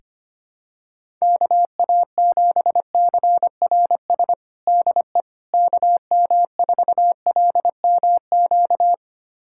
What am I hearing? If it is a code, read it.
Morse code, heard as a keyed tone, about 25 wpm, 700 Hz. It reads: KA7CRS DE KM4LMQ
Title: USA Callsign DE Callsign 25wpm